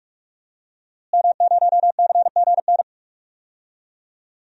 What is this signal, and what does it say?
Morse code, heard as a keyed tone, about 45 wpm, 700 Hz. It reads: M0CGN
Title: DXCC-100 Call Signs - 5 Characters - Repeated 3x - 45wpm